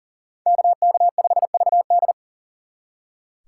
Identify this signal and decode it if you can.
Morse code, heard as a keyed tone, about 51 wpm, 700 Hz. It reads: KK5VD